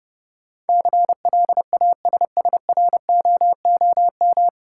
Morse code, heard as a keyed tone, about 30 wpm, 700 Hz. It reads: CLASSROOM
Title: Sentences from Top 1500 Words - New Word Drill - 30wpm